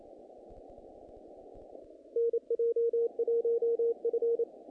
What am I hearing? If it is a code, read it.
Morse code, heard as a keyed tone, about 27 wpm, 470 Hz. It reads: NJ1F
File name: contests / naqp-cw-k5tr-op-k5pi- -2100